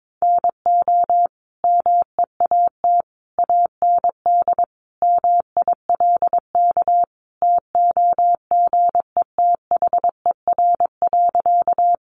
Morse code, heard as a keyed tone, about 22 wpm, 700 Hz. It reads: NO MEAT AND MILK TOGETHER.
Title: Sentences from Top 1500 Words 22wpm